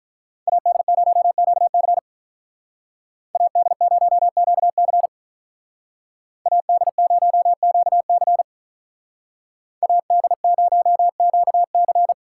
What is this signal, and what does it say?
Morse code, heard as a keyed tone, about 45 wpm, 700 Hz. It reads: AD0QC AD0QC AD0QC AD0QC